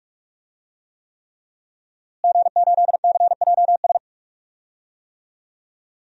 Morse code, heard as a keyed tone, about 46 wpm, 700 Hz. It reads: G8CJS
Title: DXCC-100 Call Signs - 5 Characters - Repeated 3x - 45wpm